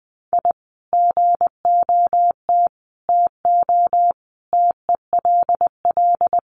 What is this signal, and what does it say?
Morse code, heard as a keyed tone, about 20 wpm, 700 Hz. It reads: I GOT TO TELL
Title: Sets of 5 Words Encore2 20wpm